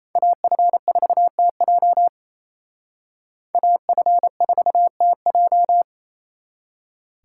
Morse code, heard as a keyed tone, about 35 wpm, 700 Hz. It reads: AF4TJ AF4TJ